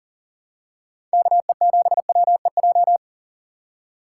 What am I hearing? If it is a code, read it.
Morse code, heard as a keyed tone, about 40 wpm, 700 Hz. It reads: KE7WEJ
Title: USA Call Signs - 6 Characters - 2x Character Spacing - 40wpm